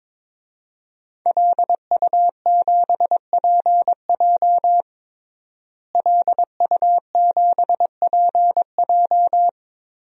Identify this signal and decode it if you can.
Morse code, heard as a keyed tone, about 21 wpm, 700 Hz. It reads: LU7PJ LU7PJ
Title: DXCC-100 Call Signs - 5 Characters - Repeated 3x - 22wpm